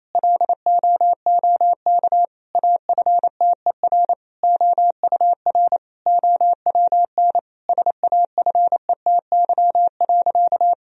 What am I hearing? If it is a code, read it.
Morse code, heard as a keyed tone, about 28 wpm, 700 Hz. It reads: LOOK AFTER OUR OWN SAFETY.
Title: Sentences from Top 1000 Words - Review Part 4 - 28wpm